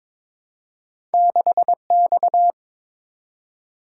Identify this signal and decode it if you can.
Morse code, heard as a keyed tone, about 22 wpm, 700 Hz. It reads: 6X
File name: practice / 22wpm / Two Letter-Number Combinations - 22wpm